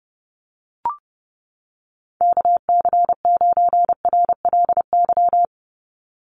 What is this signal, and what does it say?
Morse code, heard as a keyed tone, about 30 wpm, 700 Hz. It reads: KC9RLY
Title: USA Call Signs 30wpm